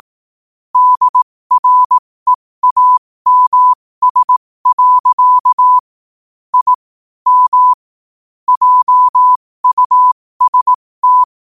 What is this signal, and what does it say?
Morse code, heard as a keyed tone, about 18 wpm, spaced as for 15 wpm, 1000 Hz. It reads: DREAMS. I M JUST
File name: qotdfiles / 15wpm